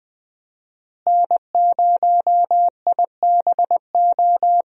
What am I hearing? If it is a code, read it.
Morse code, heard as a keyed tone, about 20 wpm, 700 Hz. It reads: N0IBO
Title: USA Call Signs - 5 Characters - Repeated 3x - 20wpm